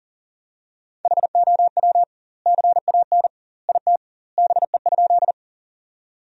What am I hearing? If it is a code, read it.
Morse code, heard as a keyed tone, about 40 wpm, 700 Hz. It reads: HOW CAN IT BE?
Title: 2x Word-Spacing - Sentences from Top 100 Words - 40wpm